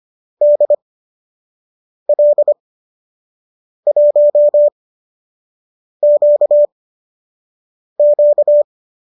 Morse code, heard as a keyed tone, about 25 wpm, spaced as for 6 wpm, 590 Hz. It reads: DL1QQ
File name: qso012 06